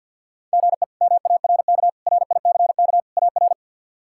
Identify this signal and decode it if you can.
Morse code, heard as a keyed tone, about 52 wpm, 700 Hz. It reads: GE MARK RICK AR
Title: K1USN SST QSO Practice - 50wpm